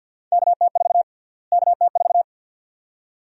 Morse code, heard as a keyed tone, about 52 wpm, 700 Hz. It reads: KT4 KT4
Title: USA Call Sign Prefixes - Repeated 2x - 50wpm